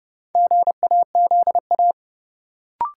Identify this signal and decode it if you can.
Morse code, heard as a keyed tone, about 32 wpm, 700 Hz. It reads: GAZA
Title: Sets of 5 Words - New Word Drill - 30wpm